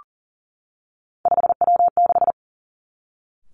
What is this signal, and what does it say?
Morse code, heard as a keyed tone, about 42 wpm, 700 Hz. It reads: 5W6
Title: Three Challenging Letter-Number Combinations 40wpm